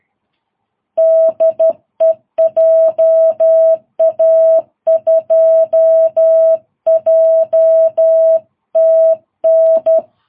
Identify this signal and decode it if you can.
Morse code, heard as a keyed tone, about 12 wpm, 660 Hz. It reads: DE JA2JTN
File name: jh8cbh / 2022cw